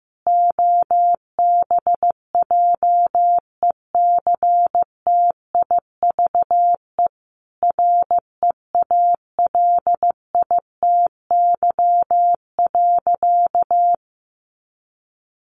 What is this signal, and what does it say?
Morse code, heard as a keyed tone, about 15 wpm, 700 Hz. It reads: OBJECTIVE REALITY.
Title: Sentences from Top 1200 Words 15wpm